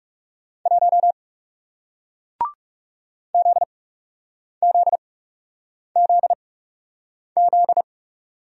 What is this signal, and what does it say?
Morse code, heard as a keyed tone, about 40 wpm, spaced as for 8 wpm, 700 Hz. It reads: 1 ZZZZ